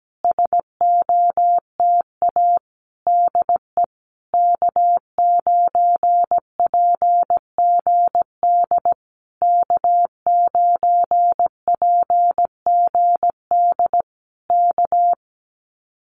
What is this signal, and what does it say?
Morse code, heard as a keyed tone, about 17 wpm, 700 Hz. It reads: SOTA DE K9PGD K9PGD K
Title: Common QSO Phrases Encore 17wpm